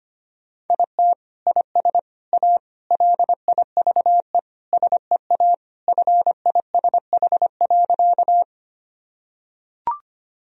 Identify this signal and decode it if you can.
Morse code, heard as a keyed tone, about 25 wpm, 700 Hz. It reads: IT IS A LIVE SEA FISH.
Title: Sentences from Top 1000 Words - Review Part 1 - 25wpm